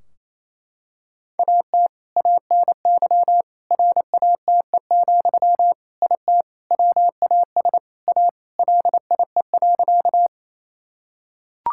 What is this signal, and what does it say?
Morse code, heard as a keyed tone, about 28 wpm, 700 Hz. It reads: AT ANY RATE, IT WAS A LIE.
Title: Sentences from Top 1600 Words 28wpm